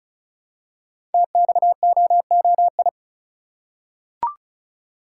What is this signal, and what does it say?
Morse code, heard as a keyed tone, about 36 wpm, 700 Hz. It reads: TXOOI
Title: Code Groups - 5 Letters - 35wpm